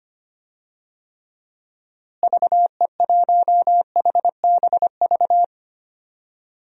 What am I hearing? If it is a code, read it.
Morse code, heard as a keyed tone, about 25 wpm, 700 Hz. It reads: VE1HBV
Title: DXCC-100 Call Signs - 6 Characters - Repeated 3x - 25wpm